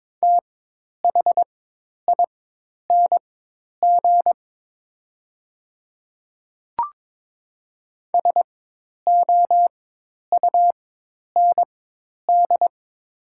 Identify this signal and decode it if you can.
Morse code, heard as a keyed tone, about 22 wpm, spaced as for 10 wpm, 700 Hz. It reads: THING SOUND
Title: 4x Character Spacing - Top 200 Words - 22wpm